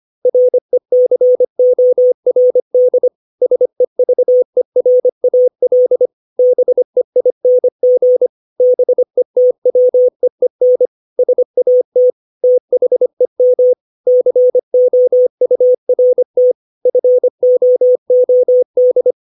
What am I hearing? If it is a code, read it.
Morse code, heard as a keyed tone, about 25 wpm, 500 Hz. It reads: RECORD SEVERAL BEING BETWEEN SAT THEM COURT FOOD
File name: 632 words / 632 Words 25WPM